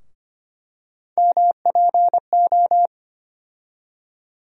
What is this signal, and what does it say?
Morse code, heard as a keyed tone, about 25 wpm, 700 Hz. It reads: MPO
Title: USA Call Sign Postfixes - 3 Letters 25wpm